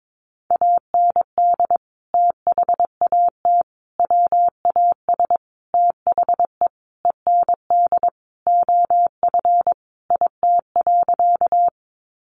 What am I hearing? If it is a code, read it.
Morse code, heard as a keyed tone, about 22 wpm, 700 Hz. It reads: AND THAT WAS THE END OF IT.
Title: Sentences from Top 200 Words 22wpm